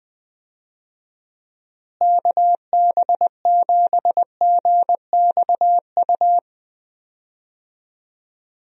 Morse code, heard as a keyed tone, about 21 wpm, 700 Hz. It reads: KB7GXU
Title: USA Call Signs - 6 Characters - Repeated 3x - 20wpm